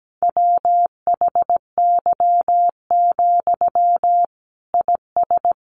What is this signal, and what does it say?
Morse code, heard as a keyed tone, about 17 wpm, 700 Hz. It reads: WHY, IS
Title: Sentences from Top 300 Words 17wpm